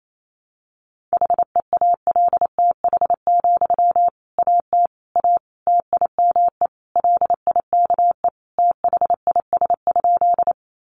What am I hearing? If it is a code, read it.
Morse code, heard as a keyed tone, about 28 wpm, 700 Hz. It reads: HEALTH, AT A TIME LIKE THIS?
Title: Sentences from Top 600 Words 28wpm